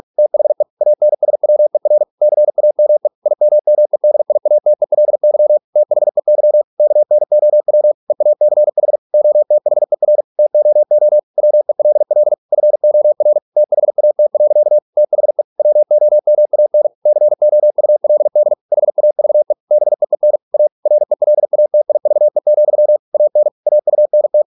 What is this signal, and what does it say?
Morse code, heard as a keyed tone, about 46 wpm, 600 Hz. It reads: THE ANSWER CAME IMMEDIATELY THEY KNOW EACH OTHER TOO WELL FOR THAT. THE WOMAN COULD HAVE BEEN A RELATIVE, AN AUNT